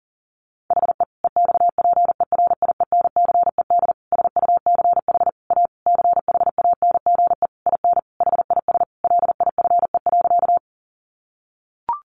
Morse code, heard as a keyed tone, about 40 wpm, 700 Hz. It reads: HE EXPERIENCED SUCH A CHANGE IN HIS LIFE.
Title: Sentences from Top 1100 Words 40wpm